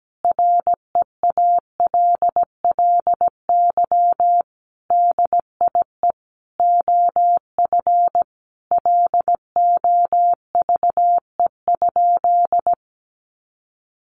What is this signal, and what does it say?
Morse code, heard as a keyed tone, about 17 wpm, 700 Hz. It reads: REALLY DIE OF LOVE?
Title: Sentences from Top 1300 Words 17wpm